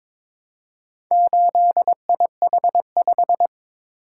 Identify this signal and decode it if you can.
Morse code, heard as a keyed tone, about 22 wpm, 700 Hz. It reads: 8IH5
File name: practice / 22wpm / Four Letter-Number Combinations - 22wpm